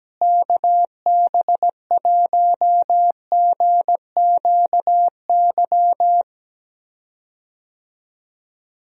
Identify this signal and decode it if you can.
Morse code, heard as a keyed tone, about 17 wpm, 700 Hz. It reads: KB1GQY
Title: USA Call Signs - 6 Characters - Repeated 3x - 17wpm